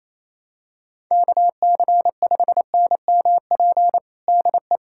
Morse code, heard as a keyed tone, about 28 wpm, 700 Hz. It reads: KC5NMP DE
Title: USA Callsign DE Callsign 28wpm